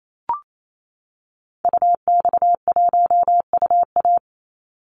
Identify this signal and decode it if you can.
Morse code, heard as a keyed tone, about 28 wpm, 700 Hz. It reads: UX1UA
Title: Call Signs - 28wpm